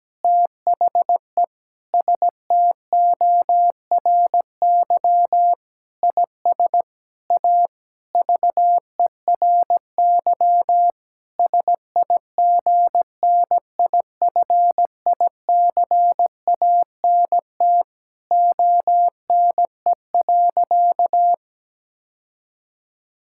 Morse code, heard as a keyed tone, about 17 wpm, 700 Hz. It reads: THE STORY IS A VERY SIGNIFICANT ONE.
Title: Sentences from Top 1000 Words - Review Part 6 - 17wpm